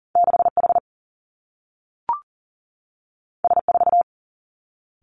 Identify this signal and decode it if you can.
Morse code, heard as a keyed tone, about 40 wpm, 700 Hz. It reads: 6 H S 4
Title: Two Challenging Letter-Number Combinations 40wpm